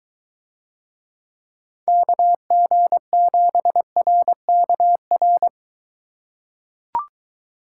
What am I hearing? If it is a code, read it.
Morse code, heard as a keyed tone, about 22 wpm, 700 Hz. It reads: KG7RKR